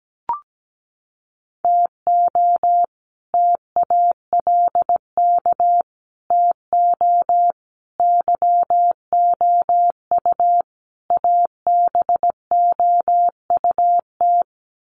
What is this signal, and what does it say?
Morse code, heard as a keyed tone, about 17 wpm, 700 Hz. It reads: TO TALK TO YOU ABOUT
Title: Sets of 5 Words Encore 17wpm